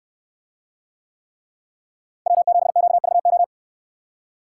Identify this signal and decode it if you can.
Morse code, heard as a keyed tone, about 64 wpm, 700 Hz. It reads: W7QVO